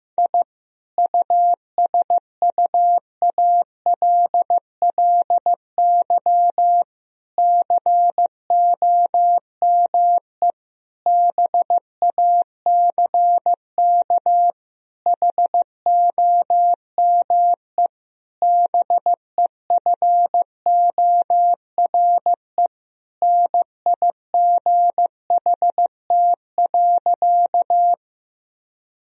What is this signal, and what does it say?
Morse code, heard as a keyed tone, about 15 wpm, 700 Hz. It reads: I USUALLY COME BACK HOME BEFORE NIGHT.
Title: Sentences from Top 1000 Words - Review Part 3 - 15wpm